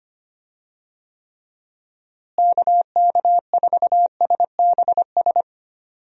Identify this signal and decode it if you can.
Morse code, heard as a keyed tone, about 25 wpm, 700 Hz. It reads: KK4SBS
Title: USA Call Signs - 6 Characters - Repeated 3x - 25wpm